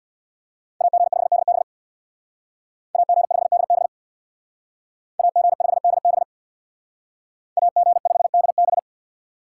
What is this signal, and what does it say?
Morse code, heard as a keyed tone, about 61 wpm, 700 Hz. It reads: AG5DB AG5DB AG5DB AG5DB
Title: Speed Racing - USA 2x2 Call Signs 50wpm